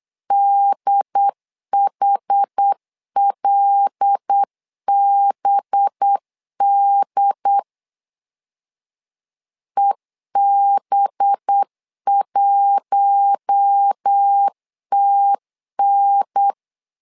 Morse code, heard as a keyed tone, about 8 wpm, 800 Hz. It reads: DHLBD EB1TN
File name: LEZIONI / 5 LBVC / 5 LB 8wpm 2816